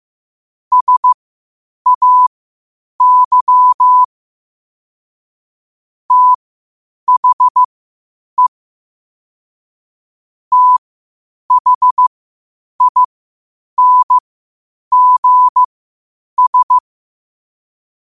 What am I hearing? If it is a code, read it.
Morse code, heard as a keyed tone, about 15 wpm, spaced as for 8 wpm, 1000 Hz. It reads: SAY THE THINGS
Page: New quotes every day in morse code at 8 Words per minute.